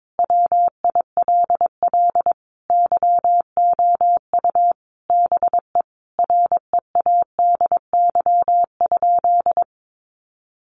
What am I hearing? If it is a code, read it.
Morse code, heard as a keyed tone, about 22 wpm, 700 Hz. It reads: WILL YOU BE READY?
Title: Sentences from Top 1200 Words 22wpm